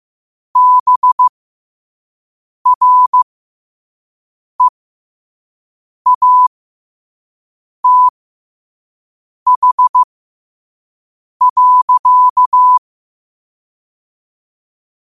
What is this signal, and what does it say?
Morse code, heard as a keyed tone, about 15 wpm, spaced as for 5 wpm, 1000 Hz. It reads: BREATH.
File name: qotdfiles / 5wpm